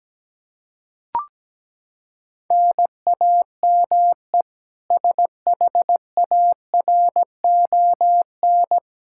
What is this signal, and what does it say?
Morse code, heard as a keyed tone, about 17 wpm, 700 Hz. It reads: NAME SHARON
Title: Practice QSO - Exchange 1 - Beginning - 17wpm